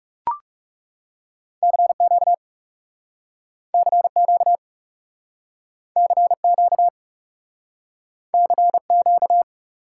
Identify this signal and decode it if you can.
Morse code, heard as a keyed tone, about 38 wpm, 700 Hz. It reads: CQ CQ CQ CQ